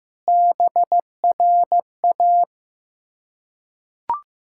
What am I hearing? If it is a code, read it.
Morse code, heard as a keyed tone, about 15 wpm, 700 Hz. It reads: BRA
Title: Ontario QSO Party Exchange 15wpm